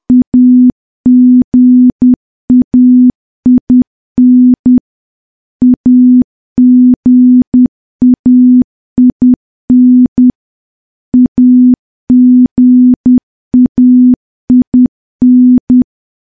Morse code, heard as a keyed tone, about 10 wpm, 250 Hz. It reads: AGAIN AGAIN AGAIN
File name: morse / morse-training-10WPM